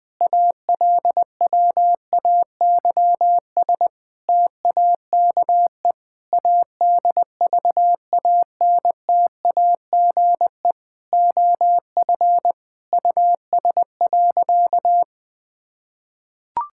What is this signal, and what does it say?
Morse code, heard as a keyed tone, about 20 wpm, 700 Hz. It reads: ALWAYS TAKE ADVANTAGE OF US.
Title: Sentences from Top 1000 Words - Review Part 6 - 20wpm